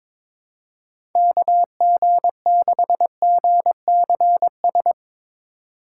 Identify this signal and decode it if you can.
Morse code, heard as a keyed tone, about 22 wpm, 700 Hz. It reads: KG6GCS
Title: USA Call Signs - 6 Characters - 4x Character Spacing - 22wpm